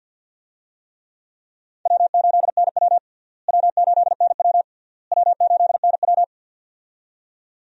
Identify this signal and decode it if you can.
Morse code, heard as a keyed tone, about 50 wpm, 700 Hz. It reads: W8NW W8NW W8NW
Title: USA Call Signs - 4 Characters - Repeated 3x - 50wpm